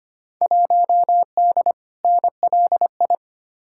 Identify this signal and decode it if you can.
Morse code, heard as a keyed tone, about 25 wpm, 700 Hz. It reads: 1D NLI
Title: Field Day Exchanges - 25wpm